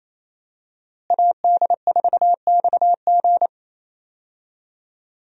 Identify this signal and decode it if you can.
Morse code, heard as a keyed tone, about 28 wpm, 700 Hz. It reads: AD4XG
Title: USA Call Signs - 5 Characters - Repeated 2x - 28wpm